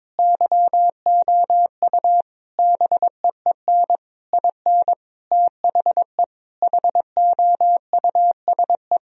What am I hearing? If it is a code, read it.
Morse code, heard as a keyed tone, about 22 wpm, 700 Hz. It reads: YOU BEEN IN THE HOUSE
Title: Sentences from Top 1000 Words - Review Part 3 - 22wpm